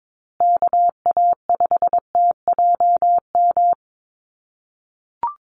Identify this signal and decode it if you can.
Morse code, heard as a keyed tone, about 22 wpm, 700 Hz. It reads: KA5TJM
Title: USA Call Signs Repeated 3x 22wpm